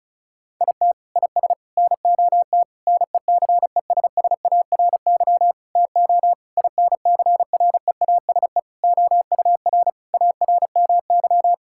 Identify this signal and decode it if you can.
Morse code, heard as a keyed tone, about 34 wpm, 700 Hz. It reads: IT IS NOT NECESSARY TO INCREASE OUR ARMY
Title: Sentences from Top 1000 Words - Review Part 6 - 35wpm